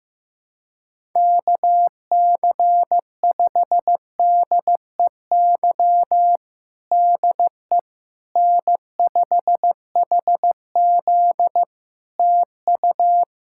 Morse code, heard as a keyed tone, about 15 wpm, 700 Hz. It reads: KC5DEY DE N5HZ TU
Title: Practice QSO - Exchange 4 - Ending - 15wpm